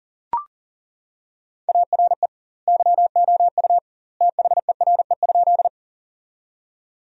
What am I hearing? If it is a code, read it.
Morse code, heard as a keyed tone, about 40 wpm, 700 Hz. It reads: ARE YOU THERE?
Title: 2x Word-Spacing - Sentences from Top 100 Words - 40wpm